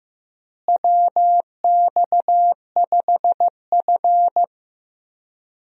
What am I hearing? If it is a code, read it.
Morse code, heard as a keyed tone, about 16 wpm, 700 Hz. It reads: WX5F
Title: USA Call Signs - 4 Characters - Repeated 3x - 15wpm